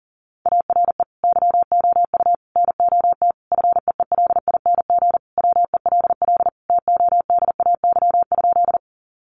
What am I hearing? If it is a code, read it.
Morse code, heard as a keyed tone, about 37 wpm, 700 Hz. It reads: ARE YOU NOT FEELING WELL TODAY?
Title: Sentences from Top 1200 Words 40wpm